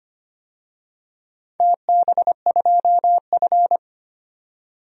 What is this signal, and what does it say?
Morse code, heard as a keyed tone, about 25 wpm, 700 Hz. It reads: TB2F
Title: DXCC-100 Call Signs - 4 Characters - 25wpm